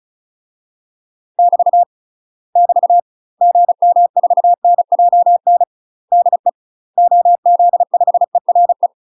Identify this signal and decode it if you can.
Morse code, heard as a keyed tone, about 34 wpm, 700 Hz. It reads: <BT> <BT> GM4NJN DE OZ5ERE
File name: UK MASTER QSOs TRAINING / 35WPM / UK MASTER QSOs 0042